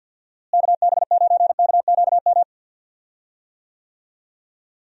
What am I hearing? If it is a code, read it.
Morse code, heard as a keyed tone, about 48 wpm, 700 Hz. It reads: KB9KQM
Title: USA Call Signs - 6 Characters - Repeated 2x - 50wpm